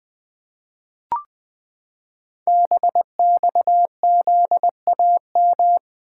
Morse code, heard as a keyed tone, about 21 wpm, 700 Hz. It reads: BXZAM
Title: Code Groups - 5 Letters - 20wpm